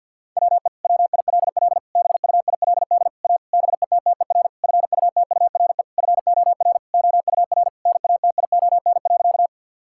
Morse code, heard as a keyed tone, about 50 wpm, 700 Hz. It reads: WE WILL BUILD A BETTER FUTURE FOR OUR NATION.
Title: Sentences from Top 1000 Words - Review Part 6 - 50wpm